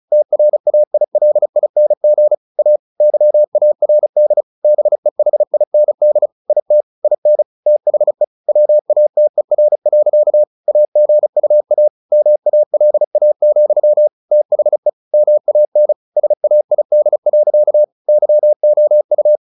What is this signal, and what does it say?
Morse code, heard as a keyed tone, about 34 wpm, 600 Hz. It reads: TRAILING A YARD BEHIND IT IN THE WATER. AGUA MALA, THE MAN SAID. YOU
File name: the old man and the sea / the old man and the sea 1 35wpm